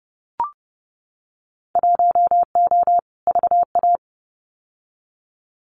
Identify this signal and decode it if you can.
Morse code, heard as a keyed tone, about 31 wpm, 700 Hz. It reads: 1O VA
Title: Winter Field Day Exchanges 30wpm